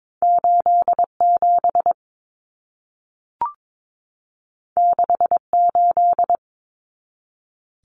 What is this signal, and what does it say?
Morse code, heard as a keyed tone, about 22 wpm, 700 Hz. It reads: 87 68
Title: Numbers - 2-Digit 22wpm